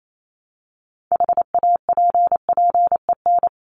Morse code, heard as a keyed tone, about 28 wpm, 700 Hz. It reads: HAPPEN
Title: ED Suffix - Top 100 Words with Base Words 28wpm